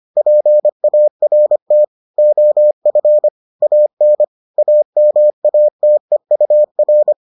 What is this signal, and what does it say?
Morse code, heard as a keyed tone, about 25 wpm, 600 Hz. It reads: PART OF AN AMATEUR